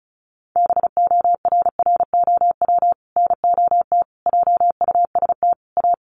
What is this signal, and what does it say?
Morse code, heard as a keyed tone, about 34 wpm, 700 Hz. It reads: BORROW NOT JUST A
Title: Sentences from Top 1500 Words 35wpm